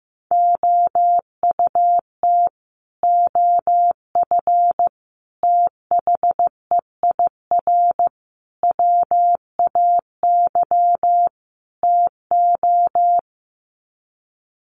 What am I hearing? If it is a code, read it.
Morse code, heard as a keyed tone, about 15 wpm, 700 Hz. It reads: OUT OF THEIR WAY TO
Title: Sets of 5 Words Encore2 15wpm